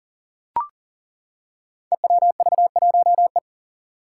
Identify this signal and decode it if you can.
Morse code, heard as a keyed tone, about 40 wpm, 700 Hz. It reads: EWV1E
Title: Five Challenging Letter-Number Combinations 40wpm